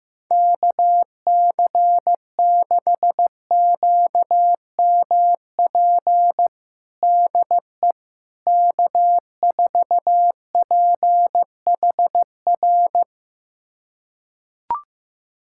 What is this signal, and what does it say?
Morse code, heard as a keyed tone, about 15 wpm, 700 Hz. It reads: KC6QMP DE K4PHR
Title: USA Callsign DE Callsign 15wpm